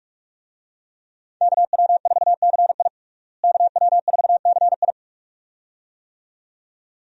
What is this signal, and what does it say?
Morse code, heard as a keyed tone, about 46 wpm, 700 Hz. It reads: KW4CI KW4CI
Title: USA Call Signs - 5 Characters - Repeated 2x - 45wpm